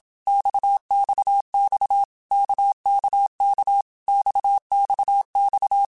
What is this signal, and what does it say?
Morse code, heard as a keyed tone, about 27 wpm, 800 Hz. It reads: XXX KKK XXX
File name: Basic Letter Groups By7Groups / RKPX / RKPX@25